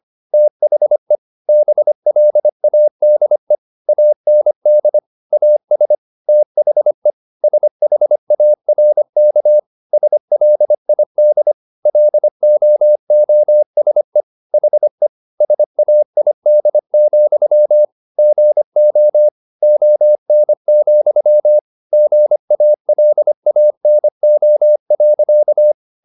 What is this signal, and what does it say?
Morse code, heard as a keyed tone, about 25 wpm, 600 Hz. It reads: THE BLADE AND AS THE SHARK SLID LOOSE HE SAID, GO ON, GALANO.